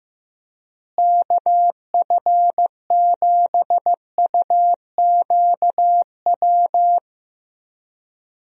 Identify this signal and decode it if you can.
Morse code, heard as a keyed tone, about 15 wpm, 700 Hz. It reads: KF7UQW
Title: USA Call Signs - 6 Characters - Repeated 2x - 15wpm